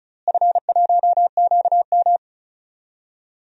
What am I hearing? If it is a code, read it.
Morse code, heard as a keyed tone, about 34 wpm, 700 Hz. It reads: F1QM
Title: DXCC-100 Call Signs - 4 Characters - Repeated 3x - 35wpm